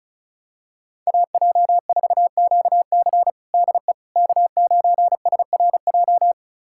Morse code, heard as a keyed tone, about 34 wpm, 700 Hz. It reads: AJ4QC DE K9SRJ
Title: USA Callsign DE Callsign - 5 Characters - 35wpm